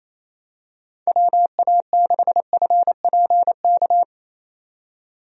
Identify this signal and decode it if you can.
Morse code, heard as a keyed tone, about 28 wpm, 700 Hz. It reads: WA6FPK
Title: USA Call Signs - 6 Characters - 28wpm